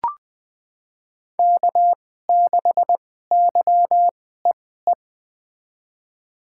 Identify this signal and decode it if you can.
Morse code, heard as a keyed tone, about 20 wpm, spaced as for 15 wpm, 700 Hz. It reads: K6YEE
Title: USA Call Signs - 2x Character Spacing - 20wpm